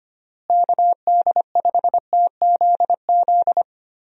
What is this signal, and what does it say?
Morse code, heard as a keyed tone, about 25 wpm, 700 Hz. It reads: KD5TZZ
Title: USA Call Signs - 6 Characters - Repeated 2x - 25wpm